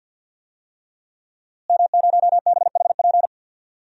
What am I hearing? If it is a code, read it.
Morse code, heard as a keyed tone, about 45 wpm, 700 Hz. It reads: M0BHP
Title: DXCC-100 Call Signs - 5 Characters - Repeated 3x - 50wpm